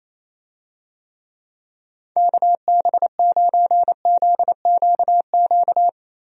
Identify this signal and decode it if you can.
Morse code, heard as a keyed tone, about 27 wpm, 700 Hz. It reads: KB9ZQQ
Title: USA Call Signs - 6 Characters - Repeated 2x - 28wpm